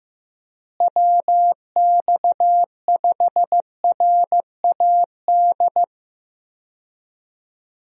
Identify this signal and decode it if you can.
Morse code, heard as a keyed tone, about 15 wpm, 700 Hz. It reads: WX5RAD
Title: USA Call Signs - 6 Characters - Repeated 3x - 15wpm